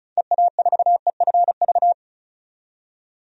Morse code, heard as a keyed tone, about 36 wpm, 700 Hz. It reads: EA4EFV
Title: DXCC-100 Call Signs - 6 Characters - 35wpm